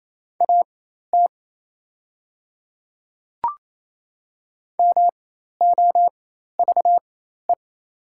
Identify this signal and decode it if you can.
Morse code, heard as a keyed tone, about 28 wpm, spaced as for 13 wpm, 690 Hz. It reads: AT MOVE
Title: 4x Character Spacing - Top 200 Words - 28wpm